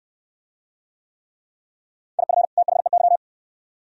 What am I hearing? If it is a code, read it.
Morse code, heard as a keyed tone, about 68 wpm, 700 Hz. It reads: IF THEY